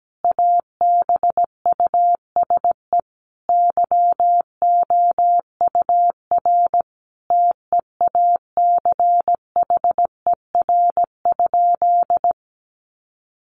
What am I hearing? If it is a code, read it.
Morse code, heard as a keyed tone, about 17 wpm, 700 Hz. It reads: ABUSE YOUR TEACHER?
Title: Sentences from Top 1300 Words 17wpm